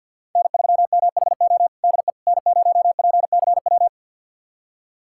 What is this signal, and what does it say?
Morse code, heard as a keyed tone, about 51 wpm, 700 Hz. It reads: N3MHO DE N0PCW
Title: USA Callsign DE Callsign - 5 Characters - 50wpm